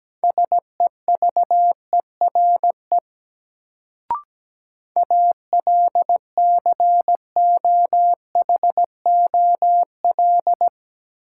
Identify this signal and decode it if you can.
Morse code, heard as a keyed tone, about 17 wpm, 700 Hz. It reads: SEVERE ALCOHOL
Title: Sentences from Top 1400 Words - New Word Drill - 17wpm